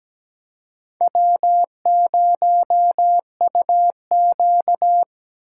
Morse code, heard as a keyed tone, about 20 wpm, 700 Hz. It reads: W0UQ
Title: Speed Racing - USA 1x2 Call Signs 17wpm